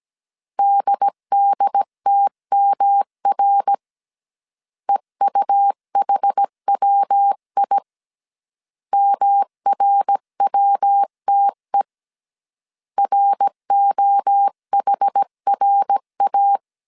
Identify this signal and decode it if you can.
Morse code, heard as a keyed tone, about 17 wpm, 800 Hz. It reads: DDTMR EUHWI MRWTE ROHRA
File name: CW / DET / 1 3 1500C S A 16wpm 2141